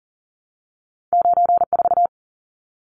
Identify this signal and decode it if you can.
Morse code, heard as a keyed tone, about 38 wpm, 700 Hz. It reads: 94
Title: Numbers - 2-Digit 40wpm